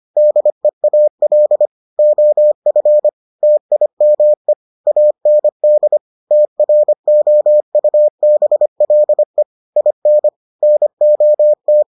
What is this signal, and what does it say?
Morse code, heard as a keyed tone, about 25 wpm, 600 Hz. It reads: DEAL OF TIME AND TROUBLE IN NOT